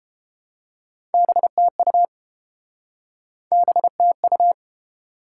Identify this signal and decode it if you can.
Morse code, heard as a keyed tone, about 30 wpm, 700 Hz. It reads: BTU BTU